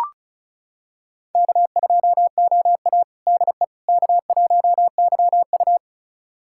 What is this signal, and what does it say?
Morse code, heard as a keyed tone, about 36 wpm, 700 Hz. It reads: K2OA DE K1YU
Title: USA Callsign DE Callsign - 4 Characters - 35wpm